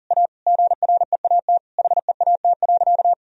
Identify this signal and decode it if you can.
Morse code, heard as a keyed tone, about 40 wpm, 700 Hz. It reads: A GREAT HEAT.
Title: Sentences from Top 1000 Words - Review Part 3 - 40wpm